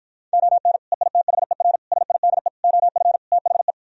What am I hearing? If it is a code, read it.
Morse code, heard as a keyed tone, about 52 wpm, 700 Hz. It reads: ON EITHER SIDE OF THE